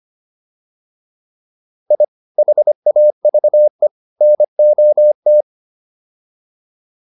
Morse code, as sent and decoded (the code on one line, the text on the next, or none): .. / .... .- ...- . / -. --- -
I HAVE NOT